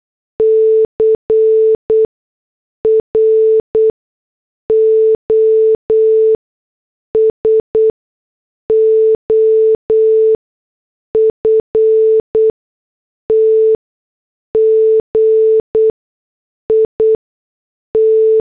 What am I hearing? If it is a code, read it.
Morse code, heard as a keyed tone, about 8 wpm, spaced as for 6 wpm, 440 Hz. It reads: CROSOFT GIT